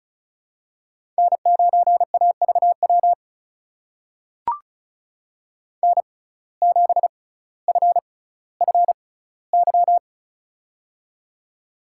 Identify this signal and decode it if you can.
Morse code, heard as a keyed tone, about 36 wpm, spaced as for 18 wpm, 700 Hz. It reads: N9AVW N7FFY
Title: USA Call Signs - 5 Characters - 6x Character Spacing - 35wpm